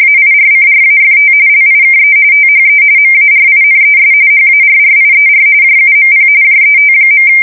RTTY, about 45.45 baud. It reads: AIRCRAFT LIKE THE MARINES HARRIER JUMP JET.
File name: fsk170 / battleshipnj